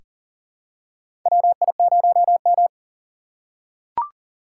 Morse code, heard as a keyed tone, about 40 wpm, 700 Hz. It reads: WI0M
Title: USA Call Signs - 4 Characters - 6x Character Spacing - 40wpm